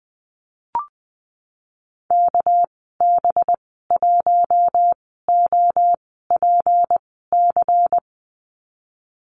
Morse code, heard as a keyed tone, about 20 wpm, spaced as for 15 wpm, 700 Hz. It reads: KB1OPC
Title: 219 Lesson 35g - Number 0 - Call Signs 20wpm